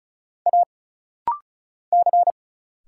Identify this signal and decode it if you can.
Morse code, heard as a keyed tone, about 34 wpm, 700 Hz. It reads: A C
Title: Single Letters - Rapid-Fire 35wpm